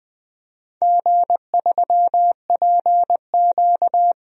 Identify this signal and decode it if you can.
Morse code, heard as a keyed tone, about 21 wpm, 700 Hz. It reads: G3PQ
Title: DXCC-100 Call Signs - 4 Characters - 20wpm